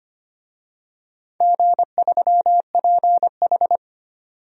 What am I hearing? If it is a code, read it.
Morse code, heard as a keyed tone, about 25 wpm, 700 Hz. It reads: G3PH
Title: DXCC-100 Call Signs - 4 Characters - 25wpm